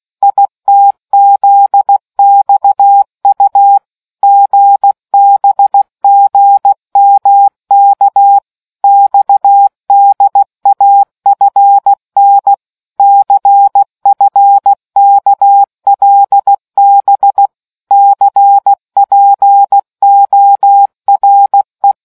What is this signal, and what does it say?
Morse code, heard as a keyed tone, about 16 wpm, 800 Hz. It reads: ITZXU GBGMK XDAFN CFKLB CPORE